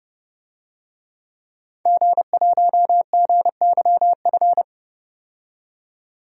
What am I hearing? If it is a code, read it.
Morse code, heard as a keyed tone, about 30 wpm, 700 Hz. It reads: G1GYF
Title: DXCC-100 Call Signs - 5 Characters - 30wpm